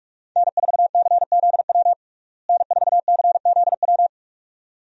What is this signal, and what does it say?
Morse code, heard as a keyed tone, about 43 wpm, 700 Hz. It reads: N4CZW N4CZW
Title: USA Call Signs - 5 Characters - Repeated 2x - 45wpm